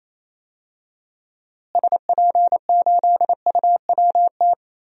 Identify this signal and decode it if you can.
Morse code, heard as a keyed tone, about 28 wpm, 700 Hz. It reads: SP8UWT
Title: DXCC-100 Call Signs - 6 Characters - Repeated 2x - 28wpm